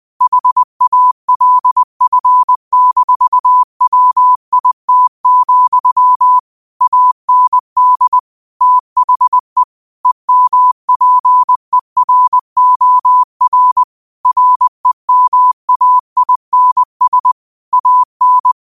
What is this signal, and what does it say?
Morse code, heard as a keyed tone, about 20 wpm, 1000 Hz. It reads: HALF-WIT, AND THE EMPEROR REMAINS AN
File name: qotdfiles / 20wpm